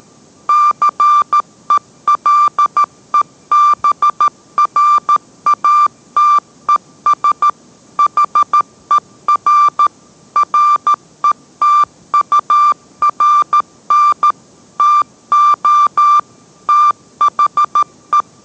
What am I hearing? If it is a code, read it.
Morse code, heard as a keyed tone, about 13 wpm, 1180 Hz. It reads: CELEBRATES HER RETURN TO THE